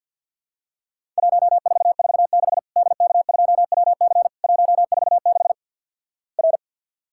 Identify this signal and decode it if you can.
Morse code, heard as a keyed tone, about 50 wpm, 700 Hz. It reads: 1446 DK2WC 146 R
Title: Worked All Europe DX Contest - QTC Exchange - 50wpm